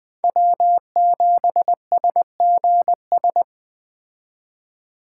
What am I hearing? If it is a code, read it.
Morse code, heard as a keyed tone, about 20 wpm, 700 Hz. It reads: W7SGS